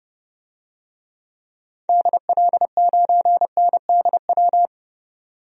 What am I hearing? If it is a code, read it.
Morse code, heard as a keyed tone, about 30 wpm, 700 Hz. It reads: DL9NDW
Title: DXCC-100 Call Signs - 6 Characters - 30wpm